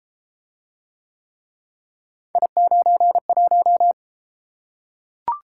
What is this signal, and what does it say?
Morse code, heard as a keyed tone, about 33 wpm, 700 Hz. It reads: I91